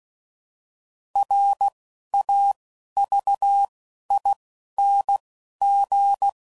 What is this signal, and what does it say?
Morse code, heard as a keyed tone, about 16 wpm, 800 Hz. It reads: R A V I N G
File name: Prefixes / By First Letter / Rs